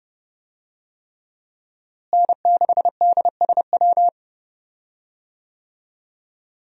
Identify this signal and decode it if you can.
Morse code, heard as a keyed tone, about 30 wpm, 700 Hz. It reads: N6DSW
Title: USA Call Signs - 5 Characters - Repeated 3x - 30wpm